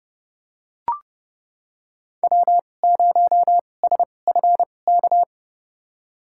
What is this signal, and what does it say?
Morse code, heard as a keyed tone, about 30 wpm, spaced as for 22 wpm, 700 Hz. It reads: W0SFK
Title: USA Call Signs - 5 Characters - 2x Character Spacing - 30wpm